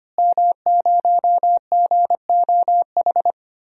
Morse code, heard as a keyed tone, about 25 wpm, 700 Hz. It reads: M0GOH
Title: DXCC-100 Call Signs - 5 Characters - Repeated 2x - 25wpm